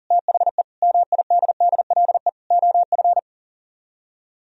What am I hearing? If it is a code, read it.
Morse code, heard as a keyed tone, about 52 wpm, spaced as for 46 wpm, 700 Hz. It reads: THE MIDDLE OF
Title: Speed-Racing - Sets of 5 Words 40wpm